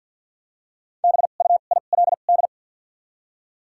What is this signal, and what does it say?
Morse code, heard as a keyed tone, about 52 wpm, spaced as for 37 wpm, 700 Hz. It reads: BUILD
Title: 2x Character Spacing - Top 200 Words - 50wpm